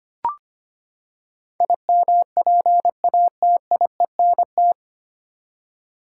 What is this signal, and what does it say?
Morse code, heard as a keyed tone, about 25 wpm, 700 Hz. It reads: IMPATIENT
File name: Nine Letter Words - 25wpm